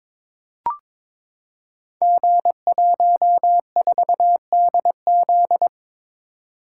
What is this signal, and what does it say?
Morse code, heard as a keyed tone, about 22 wpm, 700 Hz. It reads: G14DZ
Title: Code Groups - 5 Letter:Numbers - 22wpm